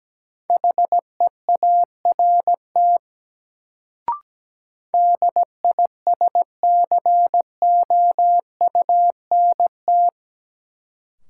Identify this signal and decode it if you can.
Morse code, heard as a keyed tone, about 17 wpm, 700 Hz. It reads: HEART DISCOUNT
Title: Sentences from Top 1200 Words - New Word Drill - 17wpm